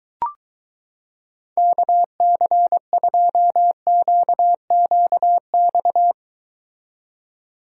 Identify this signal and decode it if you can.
Morse code, heard as a keyed tone, about 23 wpm, 700 Hz. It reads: KC2QQX